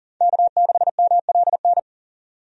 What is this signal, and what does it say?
Morse code, heard as a keyed tone, about 40 wpm, 700 Hz. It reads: K6MLN
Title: USA Call Signs - 5 Characters - 40wpm